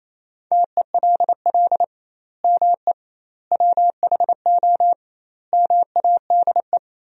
Morse code, heard as a keyed tone, about 28 wpm, 700 Hz. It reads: TELL ME WHO MADE
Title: 2x Word-Spacing - Sentences from Top 200 Words - 28wpm